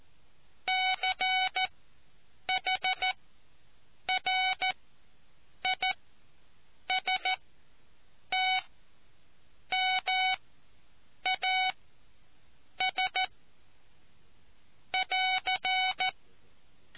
Morse code, heard as a keyed tone, about 14 wpm, spaced as for 7 wpm, 750 Hz. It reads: CHRISTMAS <AR>